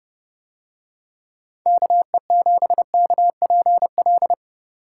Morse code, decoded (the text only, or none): KE7KPL